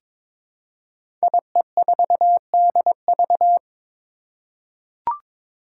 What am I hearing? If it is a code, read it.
Morse code, heard as a keyed tone, about 22 wpm, 700 Hz. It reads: IE4DV
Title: Five Challenging Letter-Number Combinations 22wpm